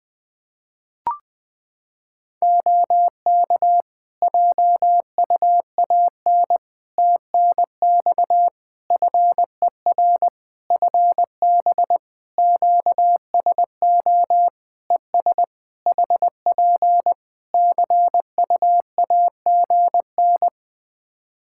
Morse code, heard as a keyed tone, about 20 wpm, 700 Hz. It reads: OK JUAN TNX FER FB QSO ES HP CUAGN
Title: Practice QSO - Exchange 4 - Ending - 20wpm